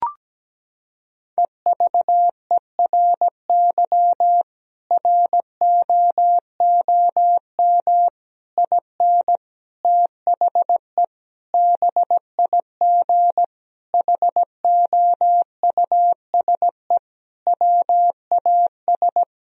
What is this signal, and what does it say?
Morse code, heard as a keyed tone, about 17 wpm, 700 Hz. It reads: EVERY ROOM IN THE BIG HOUSE WAS
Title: Sentences from Top 1000 Words - Review Part 3 - 17wpm